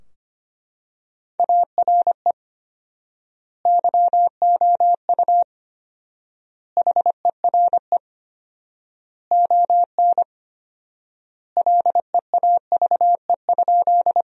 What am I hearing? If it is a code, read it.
Morse code, heard as a keyed tone, about 25 wpm, 700 Hz. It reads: ARE YOU HERE ON LEAVE?
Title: 4x Word-Spacing - Sentences from Top 400 Words - 25wpm